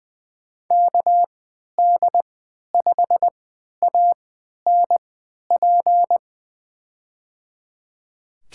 Morse code, heard as a keyed tone, about 21 wpm, spaced as for 12 wpm, 700 Hz. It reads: KD5ANP
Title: USA Call Signs - 6 Characters - 3x Character Spacing - 20wpm